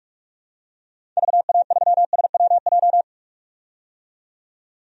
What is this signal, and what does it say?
Morse code, heard as a keyed tone, about 46 wpm, 700 Hz. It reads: VA3SWJ
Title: DXCC-100 Call Signs - 6 Characters - 45wpm